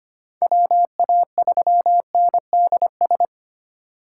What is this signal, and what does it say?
Morse code, heard as a keyed tone, about 25 wpm, 700 Hz. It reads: WA3NDS
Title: USA Call Signs - 6 Characters - 4x Character Spacing - 25wpm